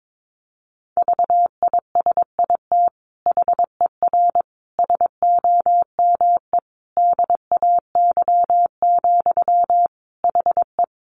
Morse code, heard as a keyed tone, about 22 wpm, 700 Hz. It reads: VISIT HER SOME DAY, HE
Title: Sentences from Top 900 Words 22wpm